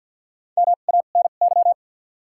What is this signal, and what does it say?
Morse code, heard as a keyed tone, about 53 wpm, 700 Hz. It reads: M A N Y
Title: 2x Character Spacing - Top 500 Words - 50wpm